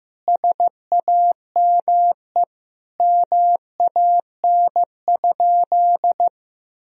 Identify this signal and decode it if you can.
Morse code, heard as a keyed tone, about 15 wpm, 700 Hz. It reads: SAME MAN?
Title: Sentences from Top 1000 Words - Review Part 4 - 15wpm